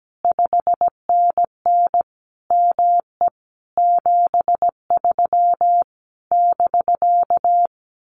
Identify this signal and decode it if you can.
Morse code, heard as a keyed tone, about 17 wpm, 700 Hz. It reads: 5NN ME 73 <BK>
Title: Common QSO Phrases Encore 17wpm